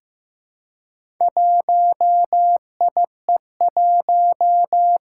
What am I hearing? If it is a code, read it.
Morse code, heard as a keyed tone, about 15 wpm, 700 Hz. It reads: 1IE1
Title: Four Challenging Letter-Number Combinations 15wpm